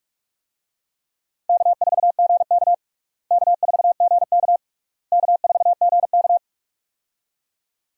Morse code, heard as a keyed tone, about 46 wpm, 700 Hz. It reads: K4GK K4GK K4GK
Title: USA Call Signs - 4 Characters - Repeated 3x - 45wpm